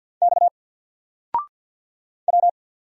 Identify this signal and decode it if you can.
Morse code, heard as a keyed tone, about 48 wpm, 700 Hz. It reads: X W
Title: Single Character (Letter:Number:Punctuation:Prosign) Alt 50wpm